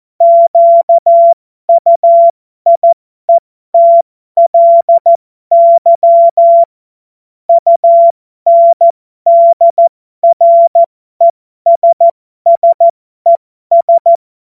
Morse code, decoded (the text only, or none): QUIETLY UNDRESSES